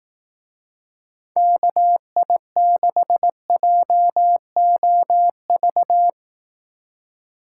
Morse code, heard as a keyed tone, about 18 wpm, 700 Hz. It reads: KI6JOV
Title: Speed Racing - USA Call Signs 15wpm